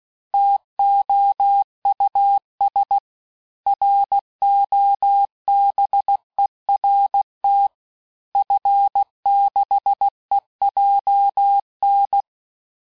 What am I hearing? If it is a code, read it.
Morse code, heard as a keyed tone, about 16 wpm, 800 Hz. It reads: TOUS ROBERT F6EJN
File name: cours-du-lundi files / cours-1580-fin-QSO